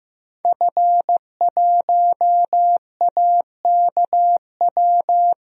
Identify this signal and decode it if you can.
Morse code, heard as a keyed tone, about 15 wpm, 700 Hz. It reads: F1AKW
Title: DXCC-100 Call Signs - 5 Characters - Repeated 3x - 15wpm